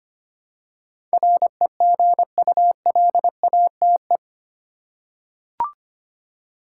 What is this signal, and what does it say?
Morse code, heard as a keyed tone, about 25 wpm, 700 Hz. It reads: REGULATE
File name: Eight Letter Words - 25wpm